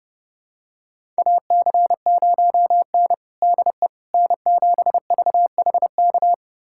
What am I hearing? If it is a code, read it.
Morse code, heard as a keyed tone, about 32 wpm, 700 Hz. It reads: AC0N DE N7VHK
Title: USA Callsign DE Callsign 30wpm